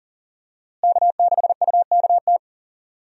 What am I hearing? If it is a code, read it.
Morse code, heard as a keyed tone, about 40 wpm, 700 Hz. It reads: K6UKT
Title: USA Call Signs - 5 Characters - Repeated 3x - 40wpm